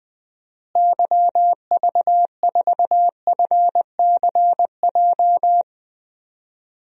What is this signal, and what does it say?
Morse code, heard as a keyed tone, about 20 wpm, 700 Hz. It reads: YV4FCJ
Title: DXCC-100 Call Signs - 6 Characters - Repeated 3x - 20wpm